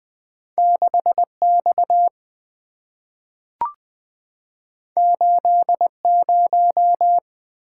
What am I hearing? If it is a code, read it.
Morse code, heard as a keyed tone, about 20 wpm, 700 Hz. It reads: 6X 80
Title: Two Letter-Number Combinations - 20wpm